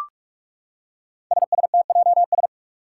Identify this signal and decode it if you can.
Morse code, heard as a keyed tone, about 42 wpm, 700 Hz. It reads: SSTJS
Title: Five Challenging Letter-Number Combinations 45wpm